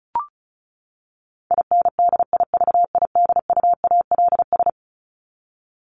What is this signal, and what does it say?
Morse code, heard as a keyed tone, about 34 wpm, 700 Hz. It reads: INDIVIDUALS
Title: ES:S Suffix - Top 100 Words with Base Words 35wpm